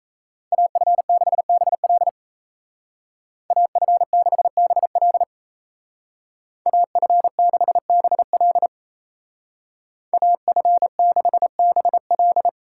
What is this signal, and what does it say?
Morse code, heard as a keyed tone, about 37 wpm, 700 Hz. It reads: AF6BL AF6BL AF6BL AF6BL